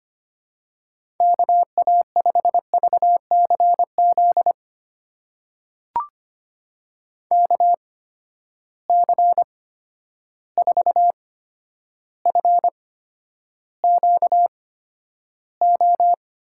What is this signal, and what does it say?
Morse code, heard as a keyed tone, about 25 wpm, spaced as for 11 wpm, 700 Hz. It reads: KA5VCZ KC4FQO
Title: USA Call Signs - 6 Characters - 8x Character Spacing - 25wpm